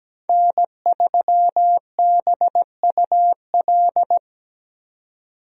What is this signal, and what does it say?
Morse code, heard as a keyed tone, about 17 wpm, 700 Hz. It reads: N3BUL
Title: USA Call Signs - 5 Characters - 17wpm